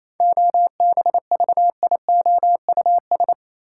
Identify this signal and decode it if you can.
Morse code, heard as a keyed tone, about 28 wpm, 700 Hz. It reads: OBVIOUS
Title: Sentences from Top 1600 Words - New Word Drill - 28wpm